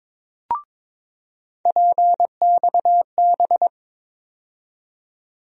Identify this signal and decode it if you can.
Morse code, heard as a keyed tone, about 22 wpm, 700 Hz. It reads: PXB
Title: USA Call Sign Postfixes - 3 Letters 22wpm